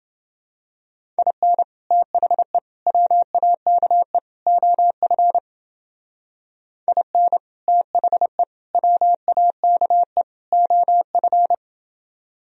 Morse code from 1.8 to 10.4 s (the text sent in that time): THE WAKE OF IN THE WAKE